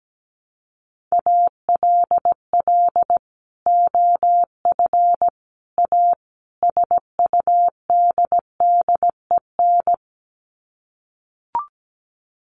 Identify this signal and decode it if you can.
Morse code, heard as a keyed tone, about 17 wpm, 700 Hz. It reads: ALL OF A SUDDEN
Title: Sets of 4 Words 17wpm